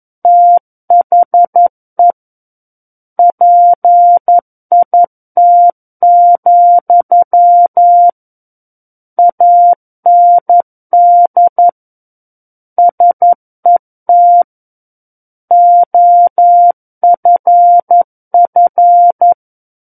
Morse code, heard as a keyed tone, about 11 wpm, 700 Hz. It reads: THE PIT, AND SET OFF
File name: Book One / 02-Chapter 2 - 11 WPM